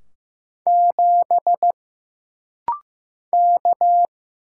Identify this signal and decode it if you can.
Morse code, heard as a keyed tone, about 15 wpm, 700 Hz. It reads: 7 K
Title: Single Letter-Number Alt - Rapid-Fire 15wpm